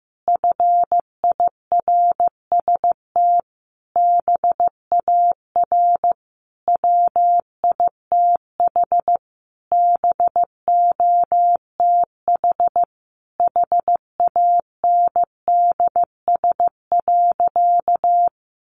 Morse code, heard as a keyed tone, about 15 wpm, 700 Hz. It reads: FIRST BAR WITH BOTH HANDS.
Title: Easy Phrases 15wpm